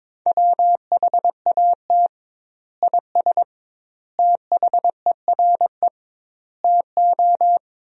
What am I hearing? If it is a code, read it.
Morse code, heard as a keyed tone, about 22 wpm, 700 Hz. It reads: WHAT IS THERE TO
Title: 2x Word-Spacing - Sentences from Top 100 Words - 22wpm